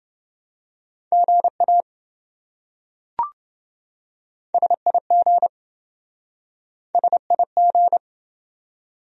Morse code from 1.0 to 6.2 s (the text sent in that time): GA SIG